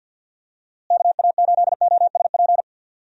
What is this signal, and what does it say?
Morse code, heard as a keyed tone, about 51 wpm, 700 Hz. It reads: KA8OSP
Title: USA Call Signs - 6 Characters - Repeated 2x - 50wpm